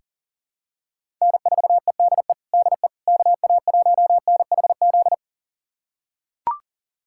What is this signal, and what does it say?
Morse code, heard as a keyed tone, about 40 wpm, 700 Hz. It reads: N4EDE DE KA1NHZ
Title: USA Callsign DE Callsign 40wpm